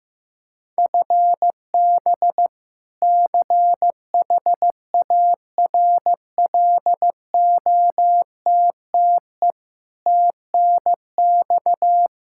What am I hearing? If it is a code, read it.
Morse code, heard as a keyed tone, about 15 wpm, 700 Hz. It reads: FB CHARLOTTE TNX
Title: Practice QSO - Exchange 3 - More Info - 15wpm